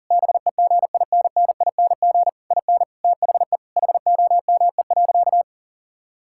Morse code, heard as a keyed tone, about 40 wpm, 700 Hz. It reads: BEGINNING IN THE HOME.
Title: Sentences from Top 1000 Words - Review Part 6 - 40wpm